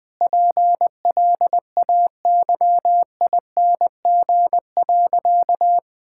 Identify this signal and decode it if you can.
Morse code, heard as a keyed tone, about 20 wpm, 700 Hz. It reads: PLAYING.
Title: Difficult Sentences with Hints - 20wpm